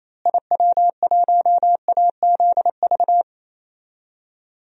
Morse code, heard as a keyed tone, about 28 wpm, 700 Hz. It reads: IW1AZV
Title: DXCC-100 Call Signs - 6 Characters - Repeated 3x - 28wpm